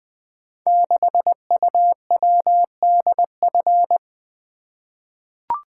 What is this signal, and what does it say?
Morse code, heard as a keyed tone, about 20 wpm, 700 Hz. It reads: 6UWDF
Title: Code Groups - 5 Letter:Numbers - 20wpm